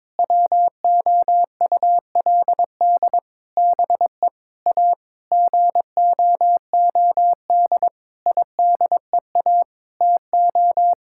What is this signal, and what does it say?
Morse code, heard as a keyed tone, about 22 wpm, 700 Hz. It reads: WOULD BE A GOOD IDEA TO
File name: Sentences with 9-Word Phrases - 22wpm